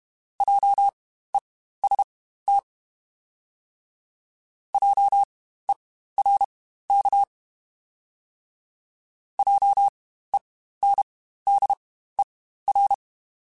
Morse code, heard as a keyed tone, about 32 wpm, spaced as for 15 wpm, 800 Hz. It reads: JEST JERK JENDER
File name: Spaced Alphas / Spaced Alphas mp3s / Js sp @30